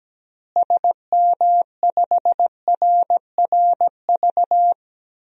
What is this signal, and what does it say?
Morse code, heard as a keyed tone, about 17 wpm, 700 Hz. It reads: SM5RRV
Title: DXCC-100 Call Signs - 6 Characters - Repeated 3x - 17wpm